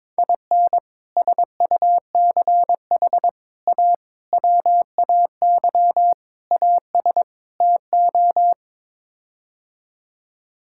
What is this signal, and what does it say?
Morse code, heard as a keyed tone, about 22 wpm, 700 Hz. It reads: IN SUCH A WAY AS TO
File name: practice / 22wpm / Sets of 6 Words - 22wpm